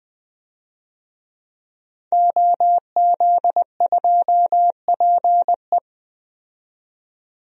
Morse code, heard as a keyed tone, about 20 wpm, 700 Hz. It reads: OZ2PE
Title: DXCC-100 Call Signs - 5 Characters - 20wpm